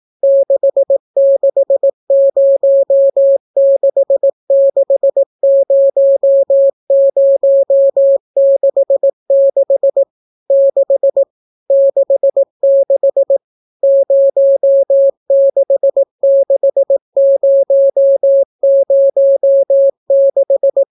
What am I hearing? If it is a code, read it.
Morse code, heard as a keyed tone, about 18 wpm, 550 Hz. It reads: 660660066 6 66 066006